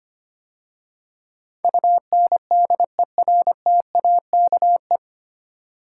Morse code, heard as a keyed tone, about 25 wpm, 700 Hz. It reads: UNDERTAKE
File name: practice / 25wpm / 9 Letter Words - First 3 Letters Given - 25wpm